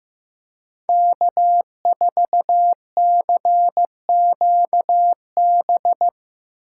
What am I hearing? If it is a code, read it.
Morse code, heard as a keyed tone, about 17 wpm, 700 Hz. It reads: K4CQB
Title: Speed Racing - USA 1x3 Call Signs 15wpm